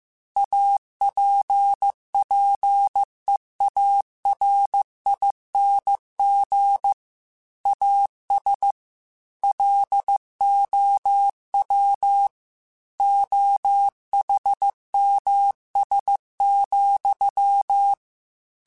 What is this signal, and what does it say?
Morse code, heard as a keyed tone, about 15 wpm, 800 Hz. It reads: APPEARING AS LOW OHMS,